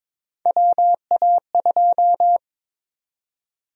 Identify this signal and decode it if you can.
Morse code, heard as a keyed tone, about 22 wpm, 700 Hz. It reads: WA2
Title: USA Call Sign Prefixes - 22wpm